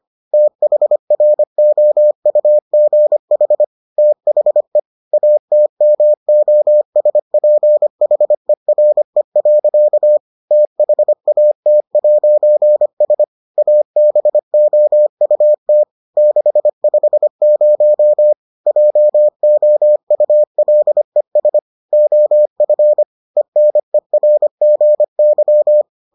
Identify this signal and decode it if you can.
Morse code, heard as a keyed tone, about 25 wpm, 600 Hz. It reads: THROUGH THE ATMOSPHERE. THAT'S ABOUT 650 JOULES OF ENERGY